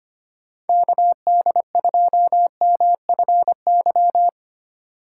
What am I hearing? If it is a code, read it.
Morse code, heard as a keyed tone, about 25 wpm, 700 Hz. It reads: KD2MFY
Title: USA Call Signs - 6 Characters - 4x Character Spacing - 25wpm